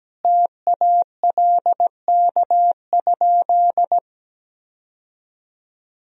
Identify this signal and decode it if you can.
Morse code, heard as a keyed tone, about 17 wpm, 700 Hz. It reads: TALK?
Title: Sentences from Top 1000 Words - Review Part 1 - 17wpm